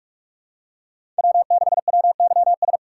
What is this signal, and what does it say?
Morse code, heard as a keyed tone, about 43 wpm, 700 Hz. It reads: W6WYS
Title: USA Call Signs - 5 Characters - 3x Character Spacing - 45wpm